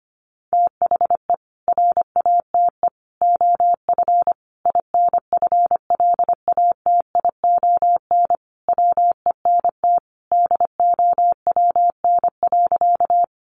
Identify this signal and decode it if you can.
Morse code, heard as a keyed tone, about 25 wpm, 700 Hz. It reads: THE RATE OF INFLATION WENT DOWN.
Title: Sentences from Top 1100 Words 25wpm